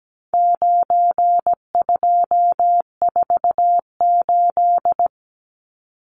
Morse code, heard as a keyed tone, about 17 wpm, 700 Hz. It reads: 9248
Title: Numbers - 4-Digit 17wpm